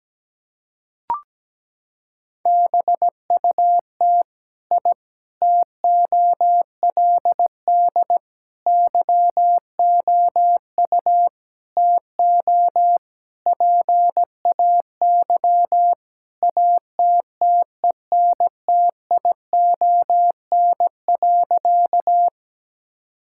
Morse code, heard as a keyed tone, about 17 wpm, 700 Hz. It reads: BUT I TOLD YOU TO PAY ATTENTION.
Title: Sentences from Top 1000 Words - Review Part 6 - 17wpm